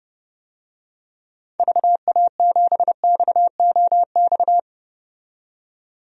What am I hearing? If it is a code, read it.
Morse code, heard as a keyed tone, about 32 wpm, 700 Hz. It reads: VA7XOX
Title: DXCC-100 Call Signs - 6 Characters - 30wpm